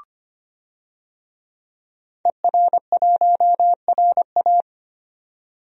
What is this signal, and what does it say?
Morse code, heard as a keyed tone, about 25 wpm, 690 Hz. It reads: ER1RA
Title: DXCC-100 Call Signs - 5 Characters - 25wpm